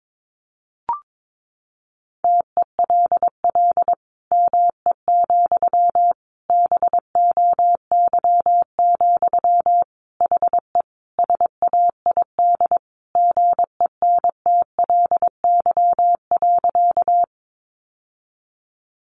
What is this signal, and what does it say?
Morse code, heard as a keyed tone, about 22 wpm, 700 Hz. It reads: TELL ME, BOY, HE SAID GENTLY.
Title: Sentences from Top 1500 Words 22wpm